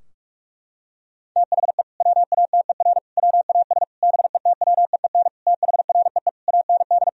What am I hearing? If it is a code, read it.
Morse code, heard as a keyed tone, about 46 wpm, 700 Hz. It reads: THE WATER WAS BETWEEN THREE AND
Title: Sentences from Top 1000 Words - Review Part 3 - 45wpm